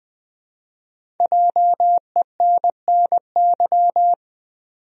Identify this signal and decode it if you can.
Morse code, heard as a keyed tone, about 20 wpm, 700 Hz. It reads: JENNY
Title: USA Common Names - Max 5 Letters - 20wpm